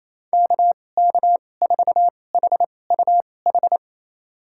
Morse code, heard as a keyed tone, about 28 wpm, spaced as for 20 wpm, 700 Hz. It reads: KK4HUH
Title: USA Call Signs - 6 Characters - 2x Character Spacing - 28wpm